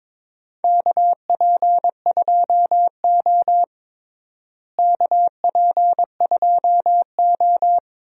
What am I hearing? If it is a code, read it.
Morse code, heard as a keyed tone, about 22 wpm, 700 Hz. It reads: KP2O KP2O
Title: DXCC-100 Call Signs - 4 Characters - Repeated 3x - 22wpm